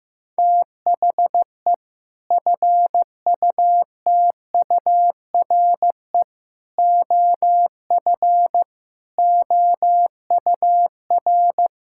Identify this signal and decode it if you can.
Morse code, heard as a keyed tone, about 15 wpm, 700 Hz. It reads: THE FUTURE OF OUR
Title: Sentences from Top 1000 Words - Review Part 6 - 15wpm